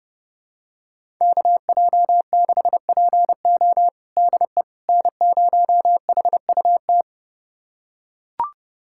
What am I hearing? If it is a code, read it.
Morse code, heard as a keyed tone, about 30 wpm, 700 Hz. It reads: KJ6PO DE N0HUT
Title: USA Callsign DE Callsign - 5 Characters - 30wpm